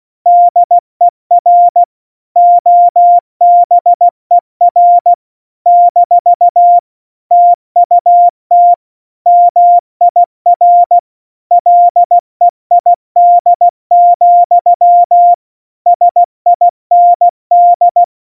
Morse code, heard as a keyed tone, about 16 wpm, 700 Hz. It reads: DER OBER - TUT MIR LEID, SIND
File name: Morsekurs / Witze / Witz 03. September 2019, 70 F80